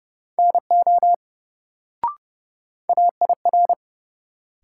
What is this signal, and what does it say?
Morse code, heard as a keyed tone, about 30 wpm, 700 Hz. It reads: NO AIR
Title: Sets of 2 Words - New Word Drill - 30wpm